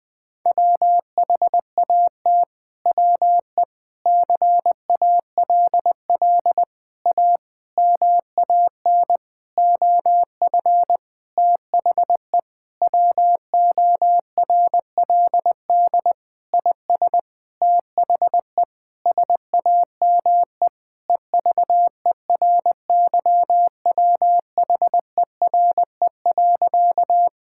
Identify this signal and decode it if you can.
Morse code, heard as a keyed tone, about 20 wpm, 700 Hz. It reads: WHAT WE CALL A MAN OF THE WORLD IS THE SAME EVERYWHERE.
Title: Sentences from Top 1600 Words 20wpm